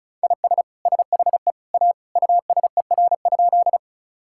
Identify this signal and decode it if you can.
Morse code, heard as a keyed tone, about 34 wpm, 700 Hz. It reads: IS SHE A USER?
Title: Sentences from Top 1000 Words - Review Part 5 - 35wpm